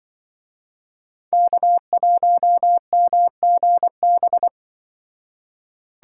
Morse code, heard as a keyed tone, about 24 wpm, 700 Hz. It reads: K1MGB
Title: Speed Racing - USA Call Signs 20wpm